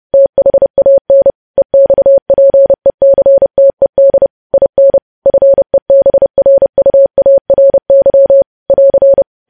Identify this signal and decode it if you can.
Morse code, heard as a keyed tone, about 30 wpm, 550 Hz. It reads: THAN EXPECTED IN FEBRUARY <AR>
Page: news headlines in morse code at 30 wpm